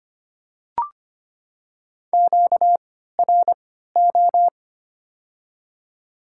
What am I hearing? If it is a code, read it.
Morse code, heard as a keyed tone, about 25 wpm, spaced as for 14 wpm, 690 Hz. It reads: QRO
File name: Top 90 QSO Elements - 3x Character Spacing - 25wpm